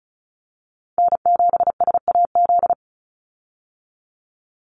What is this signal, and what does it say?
Morse code, heard as a keyed tone, about 34 wpm, 700 Hz. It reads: N7SAZ
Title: USA Call Signs 35wpm